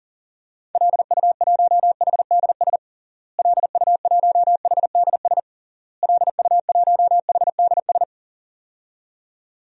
Morse code, heard as a keyed tone, about 40 wpm, 700 Hz. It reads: LU1HDS LU1HDS LU1HDS
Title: DXCC-100 Call Signs - 6 Characters - Repeated 3x - 40wpm